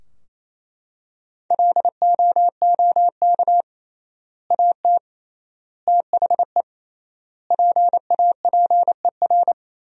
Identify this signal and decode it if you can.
Morse code, heard as a keyed tone, about 28 wpm, 700 Hz. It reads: LOOK AT THE PAPER
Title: 3x Word-Spacing - Sentences from Top 400 Words - 28wpm